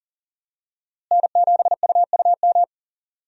Morse code, heard as a keyed tone, about 40 wpm, 700 Hz. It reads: N7UUM
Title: USA Call Signs - 5 Characters - Repeated 3x - 40wpm